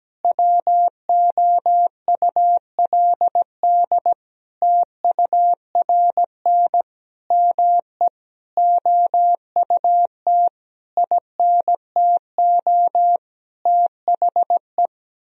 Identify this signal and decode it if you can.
Morse code, heard as a keyed tone, about 17 wpm, 700 Hz. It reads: WOULD TURN ME OUT INTO THE
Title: Sentences from Top 1000 Words - Review Part 4 - 17wpm